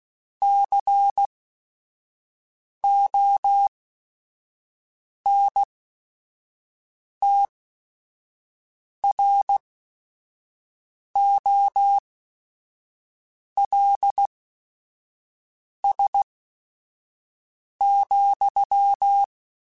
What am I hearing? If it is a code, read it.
Morse code, heard as a keyed tone, about 16 wpm, spaced as for 5 wpm, 800 Hz. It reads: CONTROLS,